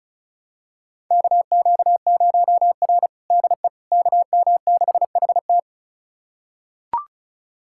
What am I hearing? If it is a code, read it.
Morse code, heard as a keyed tone, about 36 wpm, 700 Hz. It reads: KQ0R DE KM6HT
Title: USA Callsign DE Callsign 35wpm